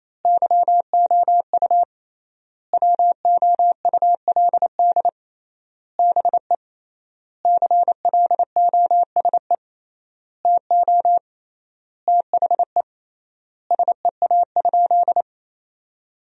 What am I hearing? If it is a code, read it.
Morse code, heard as a keyed tone, about 27 wpm, 700 Hz. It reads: YOU WOULD BE CLOSE TO THE SEA?
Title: 3x Word-Spacing - Sentences from Top 300 Words - 28wpm